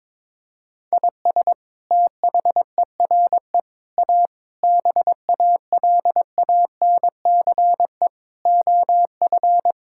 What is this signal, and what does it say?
Morse code, heard as a keyed tone, about 22 wpm, 700 Hz. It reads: IS THERE A BALANCE OF
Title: Sentences from Top 1000 Words - Review Part 6 - 22wpm